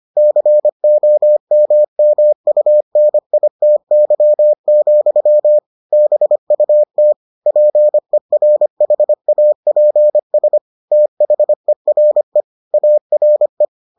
Morse code, heard as a keyed tone, about 25 wpm, 600 Hz. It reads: COMMUNITY, BUT PERHAPS THERE ARE